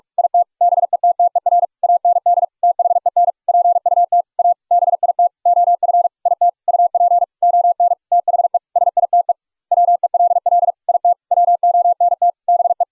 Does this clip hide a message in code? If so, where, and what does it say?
Yes: Morse code from 0.2 to 12.8 s, heard as a keyed tone, about 44 wpm, 700 Hz. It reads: IT BETTER AND THEN PUT A BIT OF IT UP ON THE SITE WELL IT WONT BE